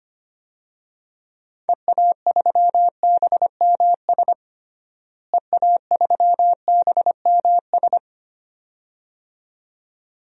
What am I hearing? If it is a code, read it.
Morse code, heard as a keyed tone, about 25 wpm, 700 Hz. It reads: EA3BMS EA3BMS
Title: DXCC-100 Call Signs - 6 Characters - Repeated 2x - 25wpm